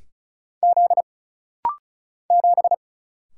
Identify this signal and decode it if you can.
Morse code, heard as a keyed tone, about 36 wpm, 700 Hz. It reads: Z 7
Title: Single Letter-Number - Rapid-Fire 35wpm